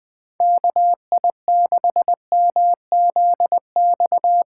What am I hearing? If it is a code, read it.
Morse code, heard as a keyed tone, about 19 wpm, 700 Hz. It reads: KI6MZX
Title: Speed Racing - USA 2x3 Call Signs 17wpm